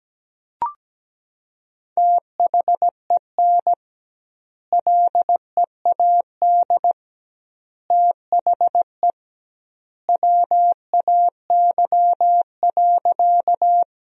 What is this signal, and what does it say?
Morse code, heard as a keyed tone, about 17 wpm, 700 Hz. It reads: THEN LEAD THE WAY.
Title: 2x Word-Spacing - Sentences from Top 500 Words - 17wpm